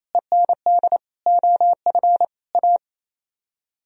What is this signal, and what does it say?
Morse code, heard as a keyed tone, about 35 wpm, 700 Hz. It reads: END OF A